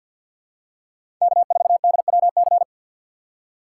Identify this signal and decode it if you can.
Morse code, heard as a keyed tone, about 52 wpm, 700 Hz. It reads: K4DWC
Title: USA Call Signs - 3x Character Spacing - 50wpm